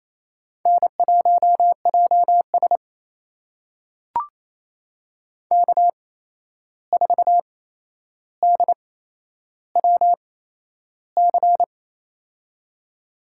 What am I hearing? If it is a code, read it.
Morse code, heard as a keyed tone, about 28 wpm, spaced as for 11 wpm, 700 Hz. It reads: N1JS K4DWC
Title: USA Call Signs - 8x Character Spacing - 28wpm